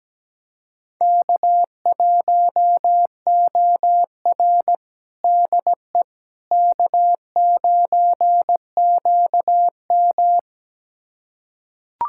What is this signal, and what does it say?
Morse code, heard as a keyed tone, about 17 wpm, 700 Hz. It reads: K1OR DE K9QM
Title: USA Callsign DE Callsign - 4 Characters - 17wpm